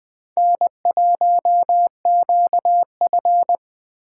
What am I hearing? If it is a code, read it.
Morse code, heard as a keyed tone, about 24 wpm, 700 Hz. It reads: N1QF
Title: Speed Racing - USA 1x2 Call Signs 20wpm